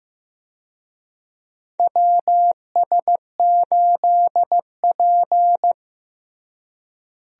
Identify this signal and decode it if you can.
Morse code, heard as a keyed tone, about 16 wpm, 700 Hz. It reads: WS8P
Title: USA Call Signs - 4 Characters - Repeated 2x - 15wpm